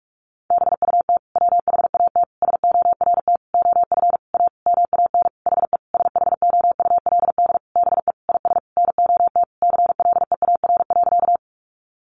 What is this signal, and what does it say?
Morse code, heard as a keyed tone, about 44 wpm, 700 Hz. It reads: BUT WHAT SORT OF A MAN HE SHOULD BE IS NOT CLEAR.
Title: Sentences from Top 1300 Words 45wpm